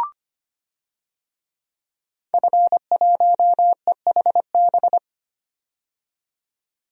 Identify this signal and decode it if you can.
Morse code, heard as a keyed tone, about 25 wpm, 700 Hz. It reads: F1EHB
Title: DXCC-100 Call Signs - 5 Characters - 25wpm